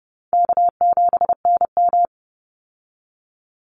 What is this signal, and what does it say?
Morse code, heard as a keyed tone, about 30 wpm, 700 Hz. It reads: K7NM
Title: Easy Call Signs Repeated 3x - 30wpm